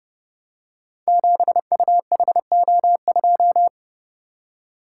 Morse code, heard as a keyed tone, about 28 wpm, 700 Hz. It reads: 7UHO2
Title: Code Groups - 5 Letter:Numbers - 30wpm